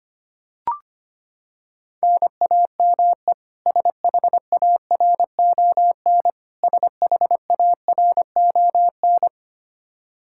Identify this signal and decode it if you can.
Morse code, heard as a keyed tone, about 25 wpm, 700 Hz. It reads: NAME SHARON SHARON
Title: Practice QSO - Exchange 1 - Beginning - 25wpm